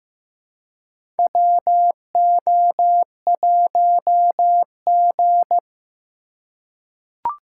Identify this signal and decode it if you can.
Morse code, heard as a keyed tone, about 15 wpm, 700 Hz. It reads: WO1G
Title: USA Call Signs - 4 Characters - 2x Character Spacing - 15wpm